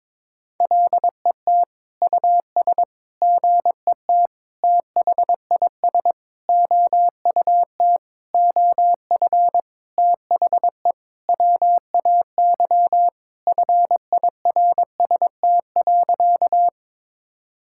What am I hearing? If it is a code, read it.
Morse code, heard as a keyed tone, about 22 wpm, 700 Hz. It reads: LET US GET THIS OUT OF THE WAY FIRST.
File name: Sentences with 7-Word Phrases - 22wpm